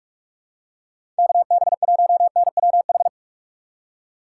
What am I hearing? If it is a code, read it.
Morse code, heard as a keyed tone, about 46 wpm, 700 Hz. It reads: KB1NWH
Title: USA Call Signs - 6 Characters - 2x Character Spacing - 45wpm